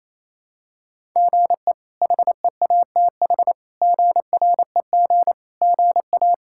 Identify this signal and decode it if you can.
Morse code, heard as a keyed tone, about 28 wpm, 700 Hz. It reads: GE HEATH GREG GA
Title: K1USN SST QSO Practice - 28wpm